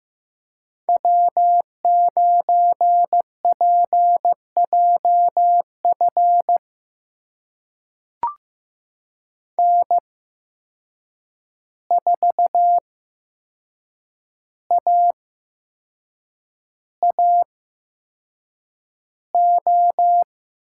Morse code, heard as a keyed tone, about 15 wpm, spaced as for 6 wpm, 700 Hz. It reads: W9PJF N4AAO
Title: USA Call Signs - 5 Characters - 8x Character Spacing - 15wpm